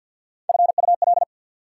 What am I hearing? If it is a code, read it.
Morse code, heard as a keyed tone, about 52 wpm, 690 Hz. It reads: FUL
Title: Prefixes and Suffixes 50wpm